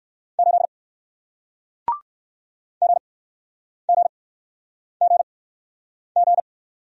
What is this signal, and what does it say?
Morse code, heard as a keyed tone, about 61 wpm, spaced as for 9 wpm, 700 Hz. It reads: 8 GGGG